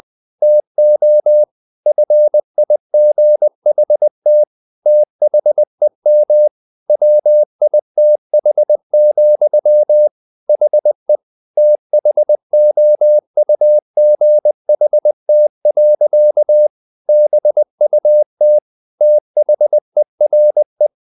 Morse code, heard as a keyed tone, about 20 wpm, 600 Hz. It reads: TO FIGHT THEM WITH, HE THOUGHT. BUT THERE
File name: the old man and the sea part 2 20wpm